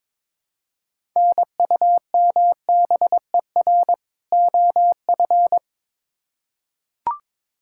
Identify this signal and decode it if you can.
Morse code, heard as a keyed tone, about 22 wpm, 700 Hz. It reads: NUMBER OF
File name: Build A Phrase - 2-Word Phrases - 22wpm